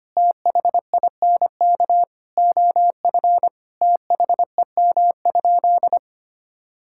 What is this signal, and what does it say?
Morse code, heard as a keyed tone, about 25 wpm, 700 Hz. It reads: THINK OF THEM?
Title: Sentences from Top 1600 Words 25wpm